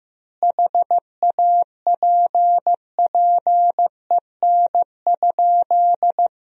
Morse code, heard as a keyed tone, about 15 wpm, 700 Hz. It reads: HAPPEN?
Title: 3x Word-Spacing - Sentences from Top 400 Words - 15wpm